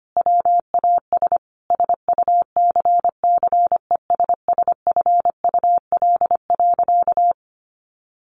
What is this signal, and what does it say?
Morse code, heard as a keyed tone, about 25 wpm, 700 Hz. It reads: WAS SUCCESSFUL.
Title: Sentences from Top 1200 Words 25wpm